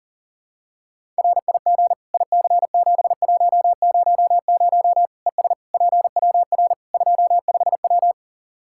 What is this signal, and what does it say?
Morse code, heard as a keyed tone, about 40 wpm, 700 Hz. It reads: RIG IC7100 ES PWR 25W
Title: K1USN SST QSO Practice 40wpm